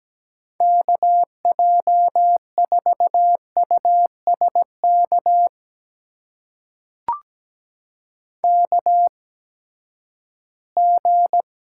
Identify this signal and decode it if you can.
Morse code, heard as a keyed tone, about 17 wpm, spaced as for 12 wpm, 700 Hz. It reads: KJ4USK KG
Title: USA Call Signs - 6 Characters - 8x Character Spacing - 17wpm